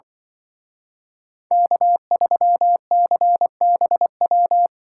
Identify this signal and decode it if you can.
Morse code, heard as a keyed tone, about 25 wpm, 700 Hz. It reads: K3CBW
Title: Speed Racing - USA 1x3 Call Signs 20wpm